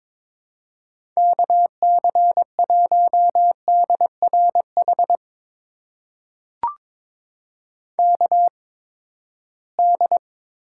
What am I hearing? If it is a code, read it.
Morse code, heard as a keyed tone, about 22 wpm, spaced as for 15 wpm, 700 Hz. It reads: KC1DRH KD
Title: USA Call Signs - 8x Character Spacing - 22wpm